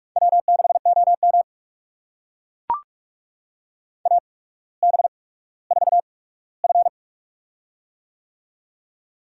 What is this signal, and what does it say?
Morse code, heard as a keyed tone, about 44 wpm, spaced as for 19 wpm, 700 Hz. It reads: W6OM AB4F
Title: USA Call Signs - 4 Characters - 8x Character Spacing - 45wpm